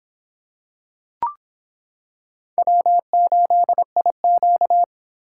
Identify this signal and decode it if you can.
Morse code, heard as a keyed tone, about 25 wpm, 700 Hz. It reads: W8IQ